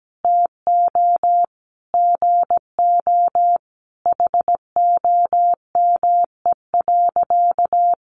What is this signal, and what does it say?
Morse code, heard as a keyed tone, about 17 wpm, 700 Hz. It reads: TO GO HOME.
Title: Sentences from Top 1400 Words 17wpm